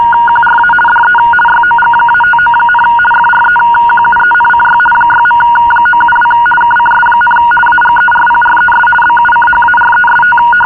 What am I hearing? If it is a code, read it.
RTTY, about 50 baud. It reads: QRA DE YOG-59, FREQUENCY 6.972 KCS, AGERPRES BUCHAREST
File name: agerpres bucharest-rtty-50 bd